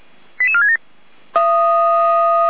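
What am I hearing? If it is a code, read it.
Five-tone selective call (ZVEI), heard as quick runs of five tones, tones of about 70 ms. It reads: 88467
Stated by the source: ZVEI five-tone selective call received by SDR.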